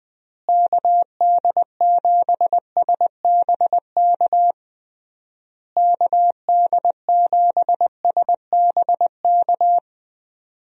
Morse code, heard as a keyed tone, about 20 wpm, 700 Hz. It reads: KD7SBK KD7SBK
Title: USA Call Signs - 6 Characters - Repeated 3x - 20wpm